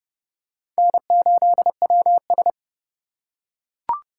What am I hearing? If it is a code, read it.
Morse code, heard as a keyed tone, about 26 wpm, 700 Hz. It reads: N8WS